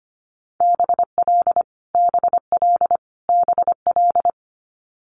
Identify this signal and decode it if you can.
Morse code, heard as a keyed tone, about 25 wpm, 700 Hz. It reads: BL BL BL
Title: Multi-letter Phonemes Repeated 3x 25wpm